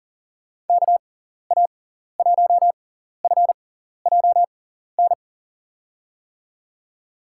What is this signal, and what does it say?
Morse code, heard as a keyed tone, about 40 wpm, spaced as for 14 wpm, 700 Hz. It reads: KA1FJN
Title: USA Call Signs - 6 Characters - 6x Character Spacing - 40wpm